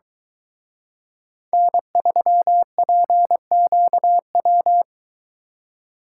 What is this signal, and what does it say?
Morse code, heard as a keyed tone, about 22 wpm, 700 Hz. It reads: N3PQW